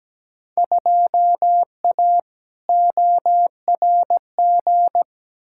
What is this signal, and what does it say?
Morse code, heard as a keyed tone, about 17 wpm, 700 Hz. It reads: 2A ORG
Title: Field Day Exchanges Encore - 17wpm